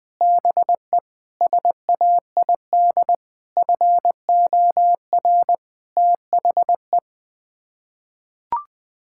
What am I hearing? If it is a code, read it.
Morse code, heard as a keyed tone, about 20 wpm, 700 Hz. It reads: BE SAID FOR THE
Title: Sets of 8 Words - 20wpm